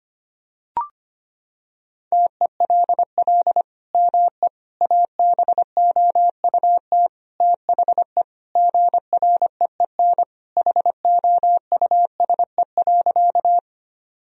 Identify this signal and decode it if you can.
Morse code, heard as a keyed tone, about 25 wpm, 700 Hz. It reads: TELL ME ABOUT THE GREEN HOUSE.
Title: Sentences from Top 1000 Words - Review Part 2 - 25wpm